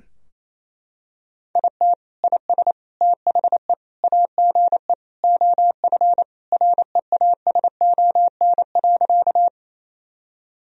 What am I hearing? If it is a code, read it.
Morse code, heard as a keyed tone, about 28 wpm, 700 Hz. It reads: IT IS THE AGE OF REASON.
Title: Sentences from Top 1000 Words - Review Part 3 - 28wpm